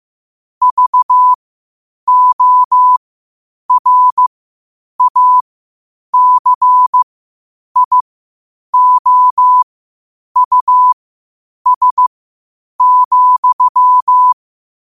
Morse code, heard as a keyed tone, about 15 wpm, spaced as for 8 wpm, 1000 Hz. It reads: VORACIOUS,
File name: qotdfiles / 8wpm